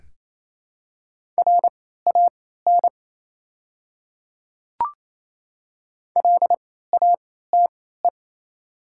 Morse code, heard as a keyed tone, about 28 wpm, spaced as for 16 wpm, 690 Hz. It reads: RAN LATE
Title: 3x Character Spacing - Top 500 Words - 28wpm